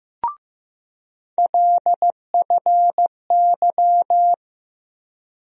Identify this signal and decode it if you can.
Morse code, heard as a keyed tone, about 15 wpm, 700 Hz. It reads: LFY
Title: USA Call Sign Postfixes - 3 Letters 15wpm